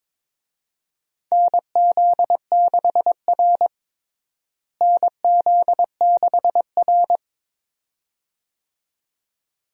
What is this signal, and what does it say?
Morse code, heard as a keyed tone, about 21 wpm, 700 Hz. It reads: NZ6R NZ6R
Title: USA Call Signs - 4 Characters - Repeated 2x - 22wpm